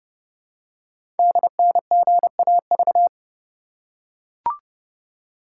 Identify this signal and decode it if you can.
Morse code, heard as a keyed tone, about 30 wpm, 700 Hz. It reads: DNGAV
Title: Code Groups - 5 Letters - 30wpm